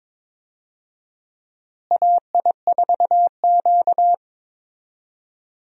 Morse code, heard as a keyed tone, about 22 wpm, 700 Hz. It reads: AI4Q
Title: USA Call Signs - 4 Characters - Repeated 2x - 22wpm